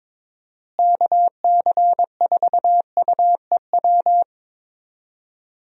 Morse code, heard as a keyed tone, about 22 wpm, 700 Hz. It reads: KC4UEW
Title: USA Call Signs - 6 Characters - Repeated 2x - 22wpm